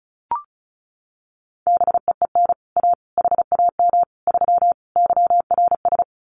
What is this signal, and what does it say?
Morse code, heard as a keyed tone, about 34 wpm, 700 Hz. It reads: BEEN A HAM 3 YRS
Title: Common QSO Phrases Encore 35wpm